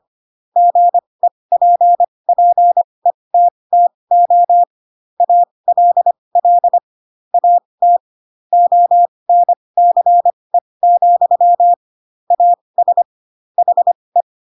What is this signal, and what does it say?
Morse code, heard as a keyed tone, about 25 wpm, spaced as for 20 wpm, 700 Hz. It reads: GEPPETTO ALL AT ONCE, AS HE
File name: Pinocchio / Pinocchio ch8 15 25 ik1hge